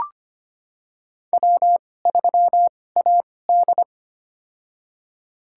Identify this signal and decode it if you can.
Morse code, heard as a keyed tone, about 25 wpm, spaced as for 18 wpm, 700 Hz. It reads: W3AD
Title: USA Call Signs - 4 Characters - 2x Character Spacing - 25wpm